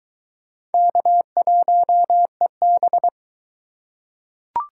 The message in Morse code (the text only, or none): K1EB